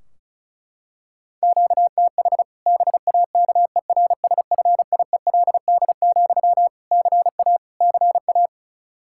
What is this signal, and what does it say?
Morse code, heard as a keyed tone, about 36 wpm, 700 Hz. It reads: QTH BAKERSFIELD, CA CA
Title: Practice QSO - Exchange 1 - Beginning - 35wpm